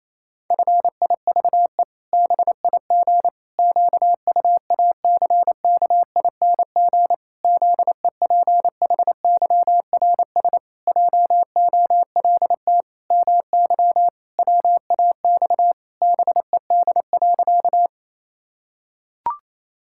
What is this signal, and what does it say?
Morse code, heard as a keyed tone, about 28 wpm, 700 Hz. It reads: FIVE BIG QUACKING ZEPHYRS JOLT MY WAX BED.
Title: Pangram Challenge v2 - 28wpm